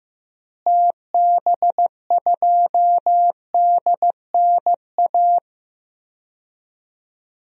Morse code, heard as a keyed tone, about 15 wpm, 700 Hz. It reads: TB2DNA
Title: DXCC-100 Call Signs - 6 Characters - Repeated 2x - 15wpm